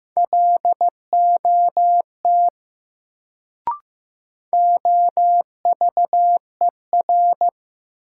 Morse code, heard as a keyed tone, about 16 wpm, 700 Hz. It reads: LOT OVER
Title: Sets of 2 Words - New Word Drill - 15wpm